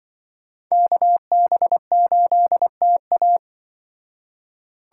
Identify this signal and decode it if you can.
Morse code, heard as a keyed tone, about 24 wpm, 700 Hz. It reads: KB8TA
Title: Speed Racing - USA 2x2 Call Signs 20wpm